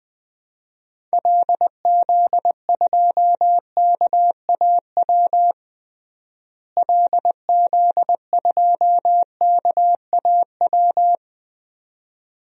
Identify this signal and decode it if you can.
Morse code, heard as a keyed tone, about 20 wpm, 700 Hz. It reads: LZ2KAW LZ2KAW
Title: DXCC-100 Call Signs - 6 Characters - Repeated 2x - 20wpm